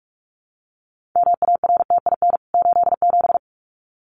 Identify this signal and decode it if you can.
Morse code, heard as a keyed tone, about 44 wpm, 700 Hz. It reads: MARTIN 87
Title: CWT Exchange 45wpm